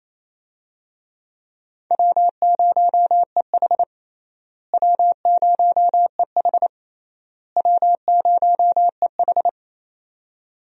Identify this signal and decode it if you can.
Morse code, heard as a keyed tone, about 28 wpm, 700 Hz. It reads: W0EH W0EH W0EH
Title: USA Call Signs - 4 Characters - Repeated 3x - 28wpm